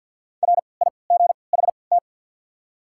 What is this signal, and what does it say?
Morse code, heard as a keyed tone, about 53 wpm, 690 Hz. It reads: R I G H T
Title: 3x Character Spacing - Top 300 Words - 50wpm